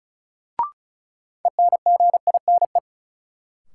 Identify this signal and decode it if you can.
Morse code, heard as a keyed tone, about 36 wpm, 700 Hz. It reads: ENGINE
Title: Sentences from Top 1500 Words - New Word Drill Alt - 35wpm